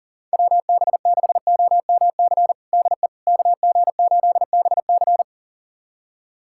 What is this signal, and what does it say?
Morse code, heard as a keyed tone, about 40 wpm, 700 Hz. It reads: WB6OMC DE KG8BC
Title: USA Callsign DE Callsign 40wpm